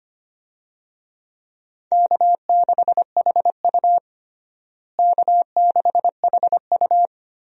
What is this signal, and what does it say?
Morse code, heard as a keyed tone, about 25 wpm, 700 Hz. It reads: K6HU K6HU
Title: USA Call Signs - 4 Characters - Repeated 3x - 25wpm